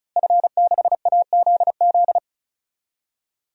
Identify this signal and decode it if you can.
Morse code, heard as a keyed tone, about 34 wpm, 700 Hz. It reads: F6AZZ
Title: DXCC-100 Call Signs - 5 Characters - Repeated 3x - 35wpm